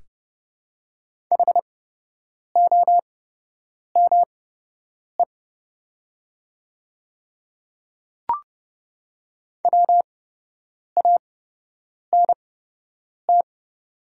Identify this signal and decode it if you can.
Morse code, heard as a keyed tone, about 29 wpm, spaced as for 8 wpm, 700 Hz. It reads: HOME WANT
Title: 8x Character Spacing - Top 300 Words - 30wpm